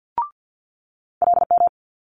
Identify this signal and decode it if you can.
Morse code, heard as a keyed tone, about 50 wpm, 700 Hz. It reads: LM
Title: Multi-letter Phonemes Quick Fire 50wpm